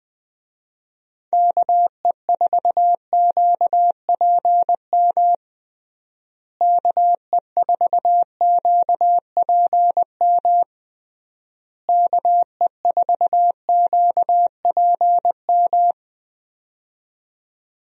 Morse code, heard as a keyed tone, about 21 wpm, 700 Hz. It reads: KE4QPM KE4QPM KE4QPM
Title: USA Call Signs - 6 Characters - Repeated 3x - 20wpm